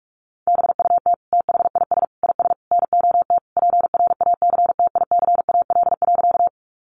Morse code, heard as a keyed tone, about 44 wpm, 700 Hz. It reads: BUT THIS IS NOT PRACTICAL.
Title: Sentences from Top 1400 Words 45wpm